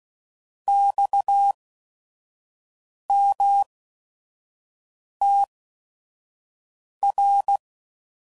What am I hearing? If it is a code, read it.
Morse code, heard as a keyed tone, about 16 wpm, spaced as for 5 wpm, 800 Hz. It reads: XMTR
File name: Mixed / Abreviations / Abreviations